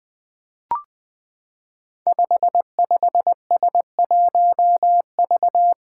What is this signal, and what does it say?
Morse code, heard as a keyed tone, about 20 wpm, 700 Hz. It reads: 55S1V
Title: Five Challenging Letter-Number Combinations 20wpm